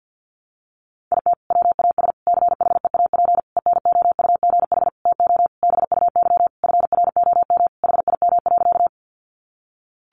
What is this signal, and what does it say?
Morse code, heard as a keyed tone, about 48 wpm, 700 Hz. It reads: IT WAS CHEAP ENOUGH TO BUY FROM HIM.
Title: Sentences from Top 1200 Words 50wpm